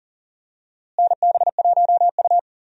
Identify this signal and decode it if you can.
Morse code, heard as a keyed tone, about 40 wpm, 700 Hz. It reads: NB1U
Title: USA Call Signs - 4 Characters - Repeated 2x - 40wpm